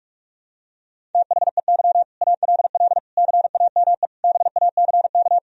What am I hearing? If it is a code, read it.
Morse code, heard as a keyed tone, about 46 wpm, 700 Hz. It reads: THEY ALL CAME BACK
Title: Sentences from Top 1000 Words - Review Part 1 - 45wpm